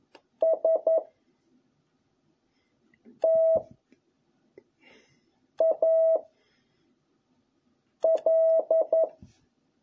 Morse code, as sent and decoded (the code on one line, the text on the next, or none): ... / - / .- / .-..
S T A L